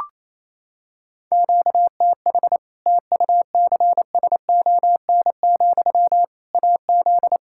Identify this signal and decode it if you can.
Morse code, heard as a keyed tone, about 28 wpm, 700 Hz. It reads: QTH TUCSON, AZ
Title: Practice QSO - Exchange 1 - Beginning - 28wpm